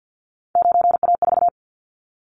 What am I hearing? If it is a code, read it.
Morse code, heard as a keyed tone, about 50 wpm, 700 Hz. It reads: 9A4
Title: Callsign Prefixes for Top 25 DXCC - 50wpm